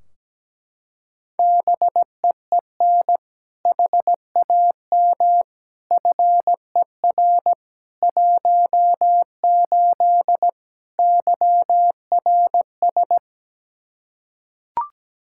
Morse code, heard as a keyed tone, about 17 wpm, 700 Hz. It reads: BEEN HAM FER 18 YRS
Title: Practice QSO - Exchange 3 - More Info - 17wpm